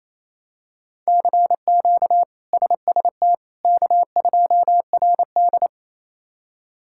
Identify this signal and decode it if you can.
Morse code, heard as a keyed tone, about 28 wpm, 700 Hz. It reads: CQ SST K2RD
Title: K1USN SST QSO Practice - 28wpm